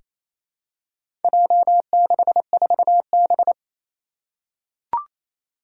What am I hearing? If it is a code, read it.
Morse code, heard as a keyed tone, about 28 wpm, 700 Hz. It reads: J64B
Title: Four Challenging Letter-Number Combinations 28wpm